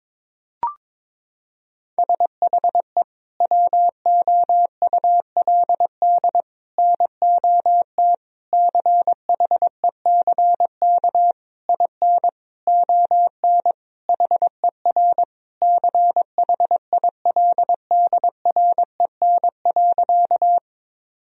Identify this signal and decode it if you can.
Morse code, heard as a keyed tone, about 22 wpm, 700 Hz. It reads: SHE WOULD NOT CHECK IN ON HER CHILDREN.
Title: Sentences from Top 1000 Words - Review Part 3 - 22wpm